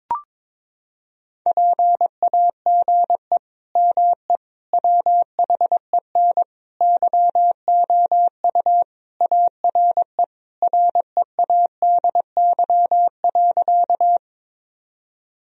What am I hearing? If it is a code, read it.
Morse code, heard as a keyed tone, about 22 wpm, 700 Hz. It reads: PAGE ME WHEN YOU ARE READY.
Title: Sentences from Top 1000 Words - Review Part 1 - 22wpm